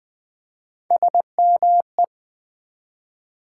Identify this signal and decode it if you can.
Morse code, heard as a keyed tone, about 20 wpm, 700 Hz. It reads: SME
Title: Build A Word - 1 Letter At A Time - 5-Letter Words - 20wpm